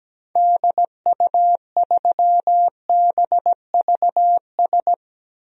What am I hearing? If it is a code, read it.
Morse code, heard as a keyed tone, about 17 wpm, 700 Hz. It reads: DU3BVS
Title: DXCC-100 Call Signs - 6 Characters - 17wpm